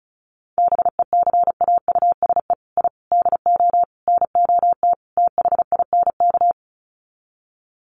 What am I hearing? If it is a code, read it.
Morse code, heard as a keyed tone, about 34 wpm, 700 Hz. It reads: BECAUSE I DO NOT THINK
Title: Sets of 5 Words Encore 35wpm